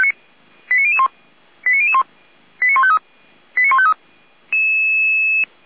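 Five-tone selective call (ZVEI), heard as quick runs of five tones, tones of about 70 ms. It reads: 89001 89001 89153 89153
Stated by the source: ZVEI five-tone call received by SDR, from audio.f85360.feuerwehr-orlamuende.org (Feuerwehr Orlamünde).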